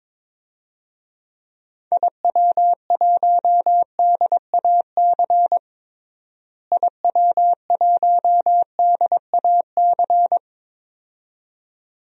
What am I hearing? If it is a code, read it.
Morse code, heard as a keyed tone, about 22 wpm, 700 Hz. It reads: IW1DAC IW1DAC
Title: DXCC-100 Call Signs - 6 Characters - Repeated 2x - 22wpm